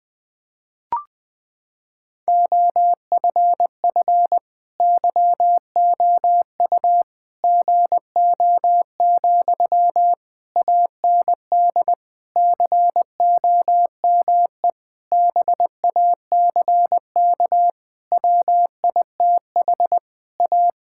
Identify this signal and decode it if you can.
Morse code, heard as a keyed tone, about 20 wpm, 700 Hz. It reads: OFF YOU GO, AND COME BACK WITH A
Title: Sentences from Top 1000 Words - Review Part 6 - 20wpm